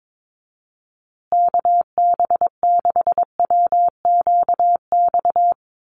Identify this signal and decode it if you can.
Morse code, heard as a keyed tone, about 22 wpm, 700 Hz. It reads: KB6WQX
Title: USA Call Signs Repeated 3x 22wpm